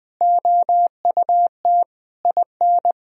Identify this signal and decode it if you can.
Morse code, heard as a keyed tone, about 20 wpm, 700 Hz. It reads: OUT IN
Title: Sentences from Top 1000 Words - Review Part 3 - 20wpm